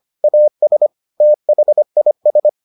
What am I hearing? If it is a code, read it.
Morse code, heard as a keyed tone, about 25 wpm, 600 Hz. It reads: AS THIS